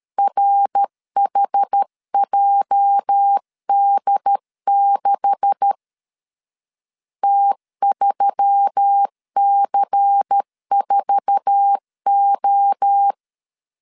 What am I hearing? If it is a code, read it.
Morse code, heard as a keyed tone, about 13 wpm, 800 Hz. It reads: RHJD6 T3C4O
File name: LEZIONI / 9 286 12wpm 2305